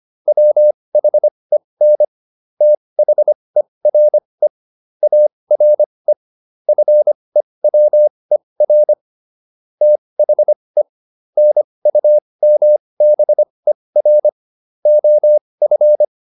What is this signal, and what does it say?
Morse code, heard as a keyed tone, about 25 wpm, spaced as for 20 wpm, 600 Hz. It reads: WHEN THERE ARE FEWER THE NUMBER OF